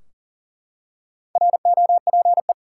Morse code, heard as a keyed tone, about 40 wpm, 700 Hz. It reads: ROPE
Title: Four Letter Words - 40wpm